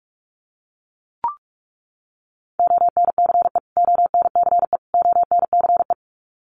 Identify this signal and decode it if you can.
Morse code, heard as a keyed tone, about 44 wpm, 700 Hz. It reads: ONCE ONCE ONCE
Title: Top 500 Words Repeated 45wpm